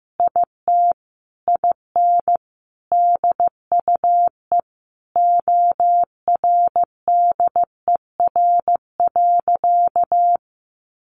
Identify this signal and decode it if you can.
Morse code, heard as a keyed tone, about 14 wpm, 700 Hz. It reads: IT IN DUE ORDER.
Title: Sentences from Top 900 Words 15wpm